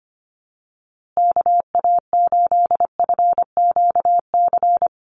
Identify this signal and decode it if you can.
Morse code, heard as a keyed tone, about 25 wpm, 700 Hz. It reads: KA8FQC
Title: USA Call Signs Repeated 3x 25wpm